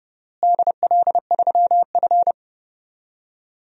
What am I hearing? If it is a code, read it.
Morse code, heard as a keyed tone, about 30 wpm, 700 Hz. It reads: DL3F
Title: DXCC-100 Call Signs - 4 Characters - Repeated 3x - 30wpm